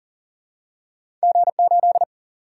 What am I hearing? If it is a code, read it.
Morse code, heard as a keyed tone, about 38 wpm, 700 Hz. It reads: G8
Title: Single-Letter Single-Number 40wpm